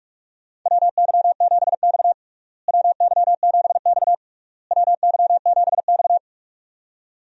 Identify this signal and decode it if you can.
Morse code, heard as a keyed tone, about 46 wpm, 700 Hz. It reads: WY7X WY7X WY7X
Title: USA Call Signs - 4 Characters - Repeated 3x - 45wpm